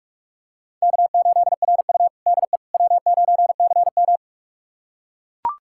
Morse code, heard as a keyed tone, about 42 wpm, 700 Hz. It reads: K8RU DE W9CM
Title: USA Callsign DE Callsign - 4 Characters - 45wpm